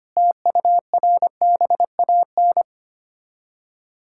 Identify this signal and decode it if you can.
Morse code, heard as a keyed tone, about 25 wpm, 700 Hz. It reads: TURBAN
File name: practice / 25wpm / Difficult Words - Speed-Racing with Hints - 25wpm